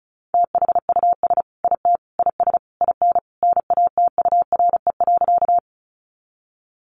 Morse code, heard as a keyed tone, about 34 wpm, 700 Hz. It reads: THUS IT IS IN NATURE.
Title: Sentences from Top 800 Words 35wpm